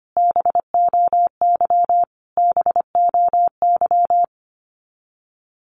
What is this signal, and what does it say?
Morse code, heard as a keyed tone, about 25 wpm, 700 Hz. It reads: BOY BOY
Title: Top 200 Words Repeated 25wpm